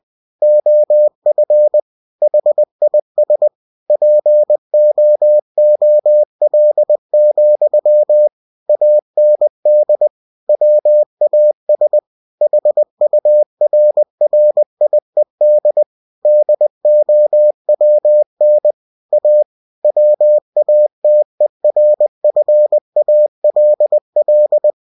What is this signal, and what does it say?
Morse code, heard as a keyed tone, about 20 wpm, 600 Hz. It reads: OF HIS POOL, AND WAS HURRIED DOWN A WATERFALL